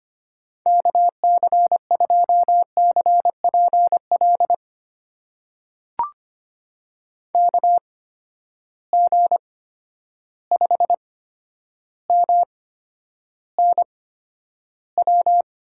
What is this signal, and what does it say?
Morse code, heard as a keyed tone, about 25 wpm, spaced as for 11 wpm, 700 Hz. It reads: KC2CPL KG5MNW
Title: USA Call Signs - 6 Characters - 8x Character Spacing - 25wpm